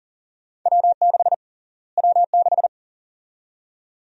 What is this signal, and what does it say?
Morse code, heard as a keyed tone, about 40 wpm, 700 Hz. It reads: W6 W6
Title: USA Call Sign Prefixes - Repeated 2x - 40wpm